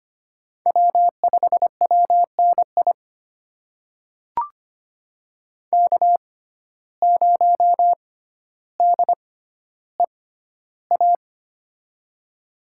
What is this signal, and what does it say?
Morse code, heard as a keyed tone, about 25 wpm, spaced as for 13 wpm, 700 Hz. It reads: W5WNI K0DEA
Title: USA Call Signs - 5 Characters - 6x Character Spacing - 25wpm